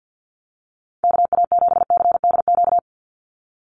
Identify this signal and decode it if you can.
Morse code, heard as a keyed tone, about 50 wpm, 700 Hz. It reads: KA7CDQ
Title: USA Call Signs 50wpm